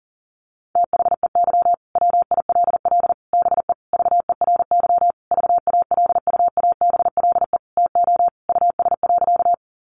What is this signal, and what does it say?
Morse code, heard as a keyed tone, about 40 wpm, 700 Hz. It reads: THEY WILL BE VERY VALUABLE TO US.
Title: Sentences from Top 1100 Words 40wpm